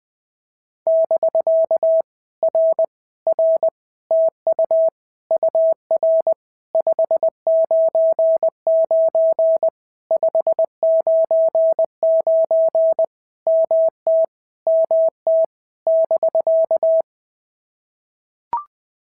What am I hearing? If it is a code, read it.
Morse code, heard as a keyed tone, about 20 wpm, 650 Hz. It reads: <BK> R R TU UR 599 599 MT MT <BK>
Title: POTA QSO Practice - 20wpm